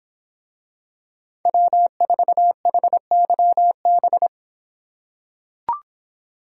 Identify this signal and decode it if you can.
Morse code, heard as a keyed tone, about 24 wpm, 700 Hz. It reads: W4HYB